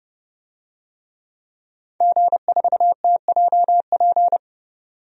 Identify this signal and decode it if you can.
Morse code, heard as a keyed tone, about 30 wpm, 700 Hz. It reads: G4TJP
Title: DXCC-100 Call Signs - 5 Characters - 30wpm